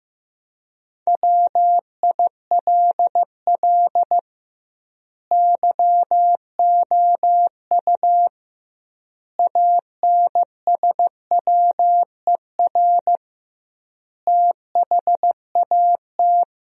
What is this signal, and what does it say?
Morse code, heard as a keyed tone, about 15 wpm, 700 Hz. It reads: WILL YOU ANSWER THAT
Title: 2x Word-Spacing - Sentences from Top 400 Words - 15wpm